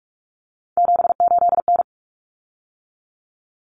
Morse code, heard as a keyed tone, about 44 wpm, 700 Hz. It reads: 78N
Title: Common Cut Numbers - 3-Digit 45wpm